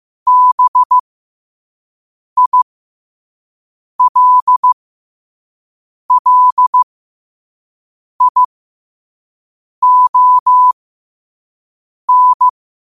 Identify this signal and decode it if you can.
Morse code, heard as a keyed tone, about 15 wpm, spaced as for 5 wpm, 1000 Hz. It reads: BILLION
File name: qotdfiles / 5wpm